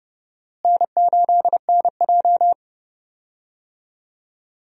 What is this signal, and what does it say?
Morse code, heard as a keyed tone, about 30 wpm, 700 Hz. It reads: N8NJ
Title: USA Call Signs - 4 Characters - Repeated 2x - 30wpm